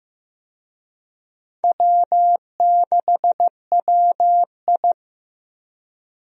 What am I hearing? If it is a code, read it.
Morse code, heard as a keyed tone, about 16 wpm, 700 Hz. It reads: W6WI
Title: USA Call Signs - 4 Characters - Repeated 2x - 15wpm